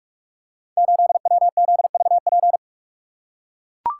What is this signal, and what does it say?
Morse code, heard as a keyed tone, about 44 wpm, 700 Hz. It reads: 8WZVP
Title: Code Groups - 5 Letter:Numbers - 45wpm